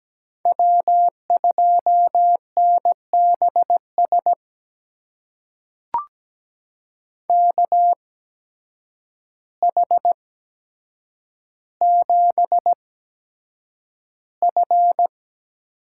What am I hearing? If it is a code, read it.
Morse code, heard as a keyed tone, about 17 wpm, spaced as for 8 wpm, 700 Hz. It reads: W2NBS KH7F
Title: USA Call Signs - 5 Characters - 8x Character Spacing - 17wpm